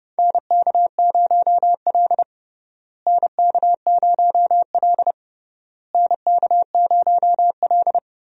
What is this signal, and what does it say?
Morse code, heard as a keyed tone, about 31 wpm, 700 Hz. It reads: NK0L NK0L NK0L
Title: USA Call Signs - 4 Characters - Repeated 3x - 30wpm